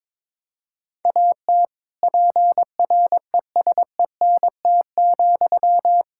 Morse code, heard as a keyed tone, about 22 wpm, 700 Hz. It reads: AT PRESENT,
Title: Sentences from Top 1000 Words - Review Part 5 - 22wpm